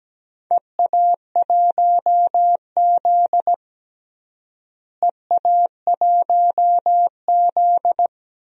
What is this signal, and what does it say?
Morse code, heard as a keyed tone, about 17 wpm, 700 Hz. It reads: EA1Z EA1Z
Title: DXCC-100 Call Signs - 4 Characters - Repeated 3x - 17wpm